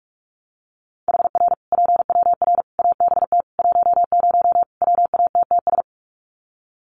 Morse code, heard as a keyed tone, about 44 wpm, 700 Hz. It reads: HR PWR ABT 10 WATTS
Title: Common QSO Phrases 45wpm